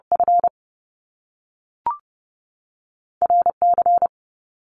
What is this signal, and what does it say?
Morse code, heard as a keyed tone, about 30 wpm, 700 Hz. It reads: F R C
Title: Top Two Letter Combinations in Words 30wpm